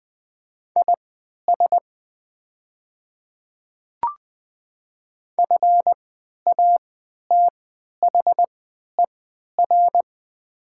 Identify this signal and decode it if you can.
Morse code, heard as a keyed tone, about 20 wpm, spaced as for 11 wpm, 700 Hz. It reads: IS FATHER
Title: 3x Character Spacing - Top 200 Words - 20wpm